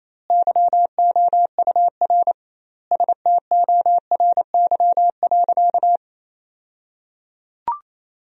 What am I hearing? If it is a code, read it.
Morse code, heard as a keyed tone, about 28 wpm, 700 Hz. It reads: YOUR STORY.
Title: 2x Word-Spacing - Sentences from Top 300 Words - 28wpm